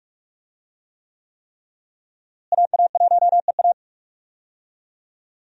Morse code, heard as a keyed tone, about 46 wpm, 700 Hz. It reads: AA1EA
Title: USA Call Signs - 5 Characters - Repeated 3x - 45wpm